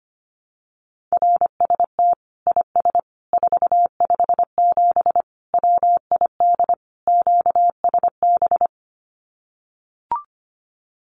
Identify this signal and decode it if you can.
Morse code, heard as a keyed tone, about 25 wpm, 700 Hz. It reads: RST IS 457 WID QSB
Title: Common QSO Phrases 25wpm